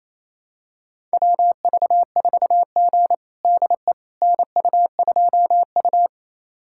Morse code, heard as a keyed tone, about 28 wpm, 700 Hz. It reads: WV4G DE NU2U
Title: USA Callsign DE Callsign - 4 Characters - 28wpm